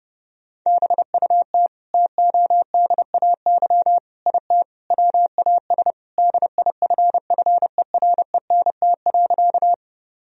Morse code, heard as a keyed tone, about 30 wpm, 700 Hz. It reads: BUT TODAY IT WAS DIFFERENT.
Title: Sentences from Top 1000 Words - Review Part 4 - 30wpm